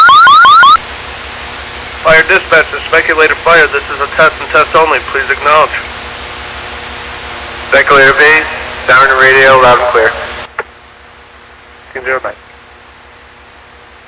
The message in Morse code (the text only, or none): none